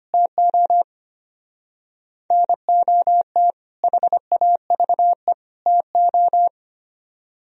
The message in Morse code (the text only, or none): TO NOT HAVE TO